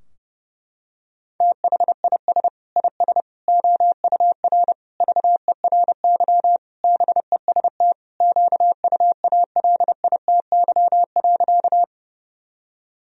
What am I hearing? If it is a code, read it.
Morse code, heard as a keyed tone, about 30 wpm, 700 Hz. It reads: THIS IS OUR VERY BEST QUALITY.
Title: Sentences from Top 1000 Words - Review Part 4 - 30wpm